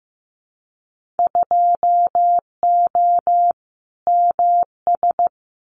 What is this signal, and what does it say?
Morse code, heard as a keyed tone, about 15 wpm, 700 Hz. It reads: 2O MS
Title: Winter Field Day Exchanges 15wpm